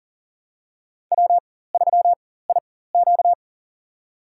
Morse code, heard as a keyed tone, about 42 wpm, spaced as for 19 wpm, 690 Hz. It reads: W3IQ
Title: USA Call Signs - 4x Character Spacing - 40wpm